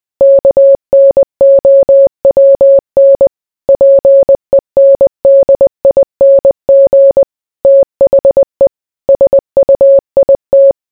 Morse code, heard as a keyed tone, about 20 wpm, 550 Hz. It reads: KNOWN PENDING THE SUIT